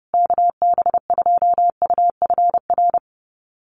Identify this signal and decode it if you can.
Morse code, heard as a keyed tone, about 30 wpm, 700 Hz. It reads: KB2UFR
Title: USA Call Signs Repeated 3x 30wpm